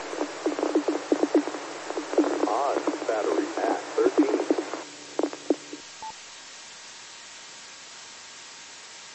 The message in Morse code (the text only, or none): none